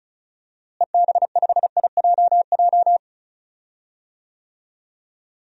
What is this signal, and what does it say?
Morse code, heard as a keyed tone, about 36 wpm, 700 Hz. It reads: EB5IJJ
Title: DXCC-100 Call Signs - 6 Characters - Repeated 2x - 35wpm